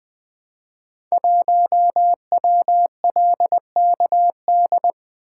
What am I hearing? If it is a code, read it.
Morse code, heard as a keyed tone, about 20 wpm, 700 Hz. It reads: 1WLKD
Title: Code Groups - 5 Letter:Numbers - 20wpm